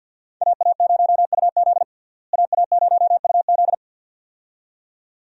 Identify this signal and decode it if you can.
Morse code, heard as a keyed tone, about 52 wpm, 700 Hz. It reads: AA0UZ AA0UZ
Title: USA Call Signs - 5 Characters - Repeated 2x - 50wpm